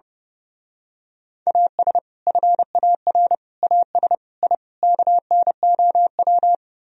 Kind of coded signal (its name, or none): Morse code